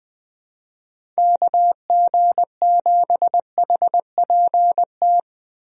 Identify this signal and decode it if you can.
Morse code, heard as a keyed tone, about 20 wpm, 700 Hz. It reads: KG7HPT
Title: USA Call Signs - 6 Characters - 20wpm